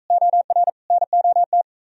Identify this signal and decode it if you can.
Morse code, heard as a keyed tone, about 44 wpm, 700 Hz. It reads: OR NOT